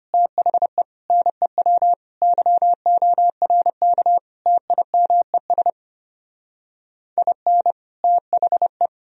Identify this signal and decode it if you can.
Morse code, heard as a keyed tone, about 33 wpm, 700 Hz. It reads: THE NEW YORK TIMES IN THE